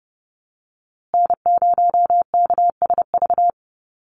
Morse code, heard as a keyed tone, about 29 wpm, 700 Hz. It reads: N0KSV
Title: USA Call Signs Repeated 2x 30wpm